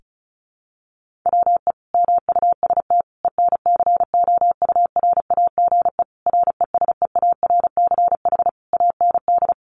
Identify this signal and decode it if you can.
Morse code, heard as a keyed tone, about 34 wpm, 700 Hz. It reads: WE MUST ENCOURAGE RESEARCH AND
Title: Sentences from Top 1400 Words 35wpm